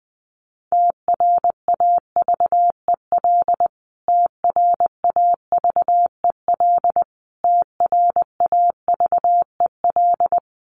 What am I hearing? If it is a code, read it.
Morse code, heard as a keyed tone, about 20 wpm, 700 Hz. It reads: TRAVEL TRAVEL TRAVEL
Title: Top 500 Words Repeated 20wpm